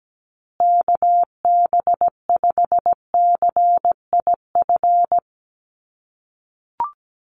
Morse code, heard as a keyed tone, about 17 wpm, 700 Hz. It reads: KB5CIF
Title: USA Call Signs Repeated 3x 17wpm